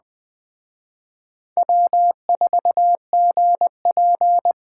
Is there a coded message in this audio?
Yes: Morse code from 1.6 to 4.5 s, heard as a keyed tone, about 20 wpm, 700 Hz. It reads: W4GP